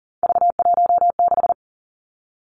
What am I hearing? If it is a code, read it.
Morse code, heard as a keyed tone, about 39 wpm, 700 Hz. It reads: V16
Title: Three Challenging Letter-Number Combinations 40wpm